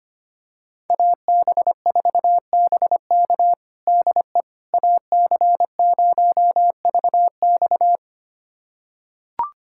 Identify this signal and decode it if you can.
Morse code, heard as a keyed tone, about 25 wpm, 700 Hz. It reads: AB4BK DE AC0VX
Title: USA Callsign DE Callsign - 5 Characters - 25wpm